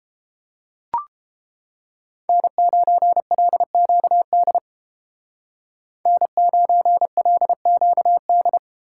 Morse code, heard as a keyed tone, about 31 wpm, 700 Hz. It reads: N9LQD N9LQD